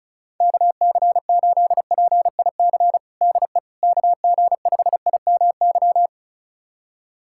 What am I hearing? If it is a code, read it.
Morse code, heard as a keyed tone, about 34 wpm, 700 Hz. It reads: KC8PIC DE KG5IMY
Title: USA Callsign DE Callsign - 6 Characters - 35wpm